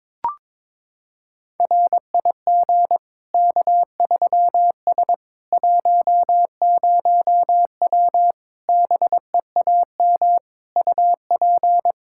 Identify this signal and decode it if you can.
Morse code, heard as a keyed tone, about 22 wpm, 700 Hz. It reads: RIG K3S 10W BEAM UP
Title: Practice QSO - Exchange 2 - Info - 22wpm